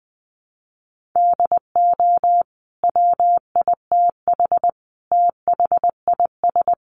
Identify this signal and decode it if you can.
Morse code, heard as a keyed tone, about 20 wpm, 700 Hz. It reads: DO WITH THIS
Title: Sentences from Top 1400 Words 20wpm